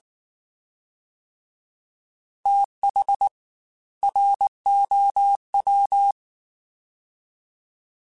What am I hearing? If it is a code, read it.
Morse code, heard as a keyed tone, about 19 wpm, 800 Hz. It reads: TH ROW
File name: TH / TH mp3s / TH Prefixes spt @18